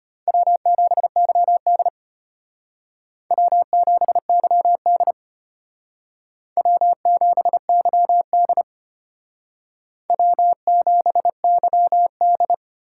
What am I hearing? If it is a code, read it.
Morse code, heard as a keyed tone, about 32 wpm, 700 Hz. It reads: W7YD W7YD W7YD W7YD